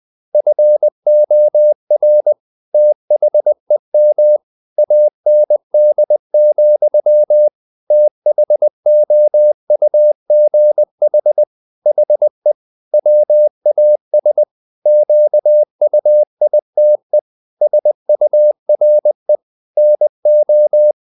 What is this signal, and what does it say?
Morse code, heard as a keyed tone, about 20 wpm, 600 Hz. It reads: FOR THEM AND, THOUGH HE WAS QUITE SURE NO
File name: the old man and the sea / the old man and the sea part 1 20wpm